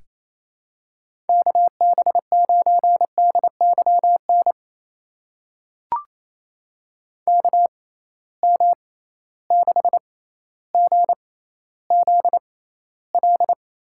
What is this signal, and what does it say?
Morse code, heard as a keyed tone, about 28 wpm, spaced as for 14 wpm, 700 Hz. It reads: KB9DYN KM6GZL
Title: USA Call Signs - 6 Characters - 6x Character Spacing - 28wpm